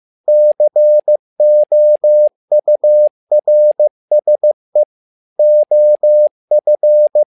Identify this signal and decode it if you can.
Morse code, heard as a keyed tone, about 15 wpm, 600 Hz. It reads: COURSE OF
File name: learning center / tower trouble / tower trouble 15wpm